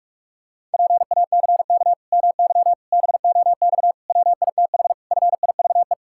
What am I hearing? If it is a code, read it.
Morse code, heard as a keyed tone, about 41 wpm, 700 Hz. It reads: PACK MY BOX WITH FIVE
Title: Pangram Challenge v2 - 45wpm